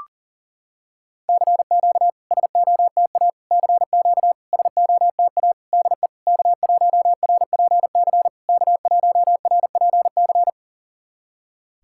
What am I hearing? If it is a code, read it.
Morse code, heard as a keyed tone, about 40 wpm, 690 Hz. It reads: CQ SOTA CQ SOTA DE K1RPC K1RPC
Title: SOTA QSO Practice 40wpm